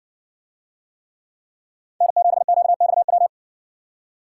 Morse code, heard as a keyed tone, about 60 wpm, 700 Hz. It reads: N7QXW